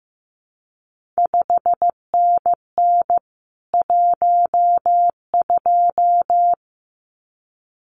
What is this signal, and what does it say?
Morse code, heard as a keyed tone, about 15 wpm, 700 Hz. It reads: 5NN 12
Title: CQ WW DX Contest Exchange Encore - 15wpm